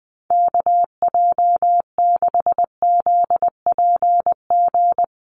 Morse code, heard as a keyed tone, about 20 wpm, 700 Hz. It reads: KJ6ZPG
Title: USA Call Signs Repeated 3x 20wpm